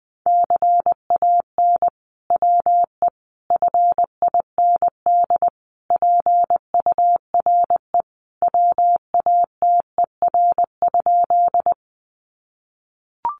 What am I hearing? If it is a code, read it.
Morse code, heard as a keyed tone, about 20 wpm, 700 Hz. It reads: CAN WE FIND PURE WATER?
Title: Sentences from Top 1500 Words 20wpm